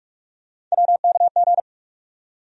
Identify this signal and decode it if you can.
Morse code, heard as a keyed tone, about 47 wpm, 700 Hz. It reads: WKG
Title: USA Call Sign Postfixes - 3 Letters 45wpm